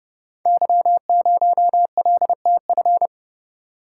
Morse code, heard as a keyed tone, about 30 wpm, 700 Hz. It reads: Y0LTF
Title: Code Groups - 5 Letter:Numbers - 30wpm